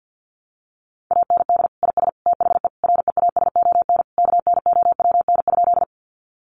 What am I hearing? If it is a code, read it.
Morse code, heard as a keyed tone, about 50 wpm, 700 Hz. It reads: AND IS THE REASON KNOWN?
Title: Sentences from Top 700 Words 50wpm